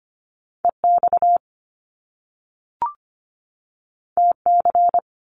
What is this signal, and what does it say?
Morse code, heard as a keyed tone, about 25 wpm, 700 Hz. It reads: EX TC
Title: Top Two Letter Combinations in Words 25wpm